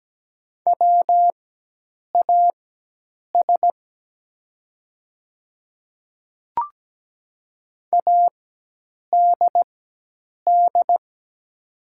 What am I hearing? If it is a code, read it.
Morse code, heard as a keyed tone, about 17 wpm, spaced as for 8 wpm, 700 Hz. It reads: WAS ADD
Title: 4x Character Spacing - Top 200 Words - 17wpm